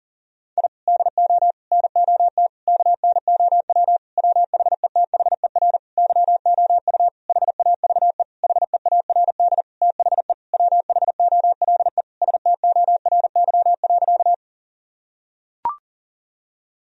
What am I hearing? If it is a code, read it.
Morse code, heard as a keyed tone, about 37 wpm, 700 Hz. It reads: I DO NOT KNOW WHETHER YOU HAVE HEARD THE WHOLE STORY.
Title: Sentences from Top 1000 Words - Review Part 3 - 40wpm